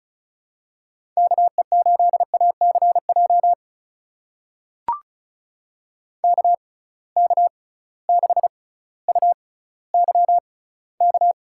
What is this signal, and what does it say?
Morse code, heard as a keyed tone, about 36 wpm, spaced as for 18 wpm, 700 Hz. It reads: KE8ACJ KK6UYK
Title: USA Call Signs - 6x Character Spacing - 35wpm